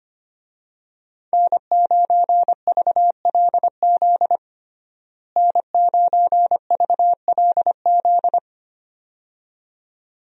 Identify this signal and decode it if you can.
Morse code, heard as a keyed tone, about 25 wpm, 700 Hz. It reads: N9VLZ N9VLZ
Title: USA Call Signs - 5 Characters - Repeated 2x - 25wpm